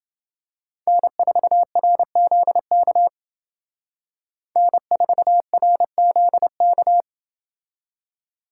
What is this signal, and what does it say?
Morse code, heard as a keyed tone, about 30 wpm, 700 Hz. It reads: N4RZK N4RZK